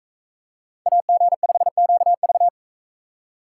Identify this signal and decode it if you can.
Morse code, heard as a keyed tone, about 39 wpm, 700 Hz. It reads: AG5QV